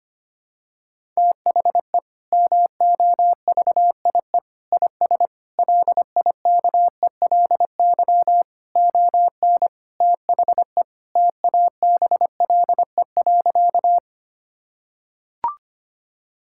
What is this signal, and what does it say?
Morse code, heard as a keyed tone, about 25 wpm, 700 Hz. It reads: THE MOVIE IS LIKELY ON THE TABLE.
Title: Sentences from Top 1000 Words - Review Part 5 - 25wpm